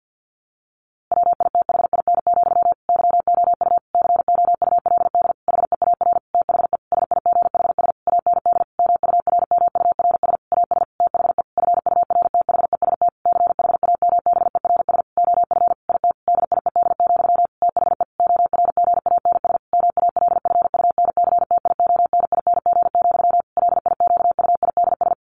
Morse code, heard as a keyed tone, about 48 wpm, 700 Hz. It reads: WITHIN, YOU COULD HEAR THE SIGHS AND MURMURS AS THE FURTHEST CHAMBERS OF IT DIED, THE ORGANS MALFUNCTIONING, LIQUIDS
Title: A Sound Of Thunder 50wpm